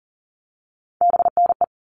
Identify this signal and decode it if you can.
Morse code, heard as a keyed tone, about 40 wpm, 700 Hz. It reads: BNE
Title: All Cut Numbers - 3-Digit 40wpm